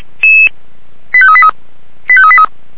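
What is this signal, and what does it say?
Five-tone selective call (ZVEI), heard as quick runs of five tones, tones of about 70 ms. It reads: 85383 85383
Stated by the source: ZVEI five-tone selective call received on an SDR.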